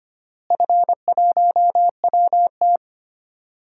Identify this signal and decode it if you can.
Morse code, heard as a keyed tone, about 25 wpm, 700 Hz. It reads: F1WT
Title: DXCC-100 Call Signs - 4 Characters - Repeated 3x - 25wpm